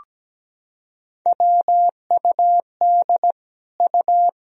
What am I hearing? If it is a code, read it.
Morse code, heard as a keyed tone, about 17 wpm, 700 Hz. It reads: WUD U
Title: Sentences from Top 200 Words and Abbreviations 17wpm